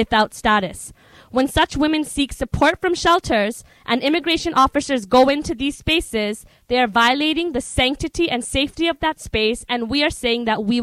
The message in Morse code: none

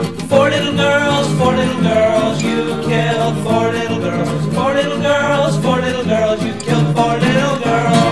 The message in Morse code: none

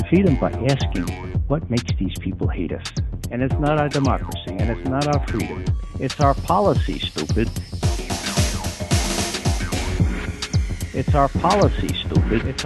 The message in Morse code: none

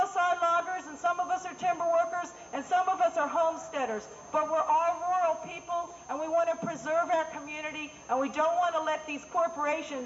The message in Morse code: none

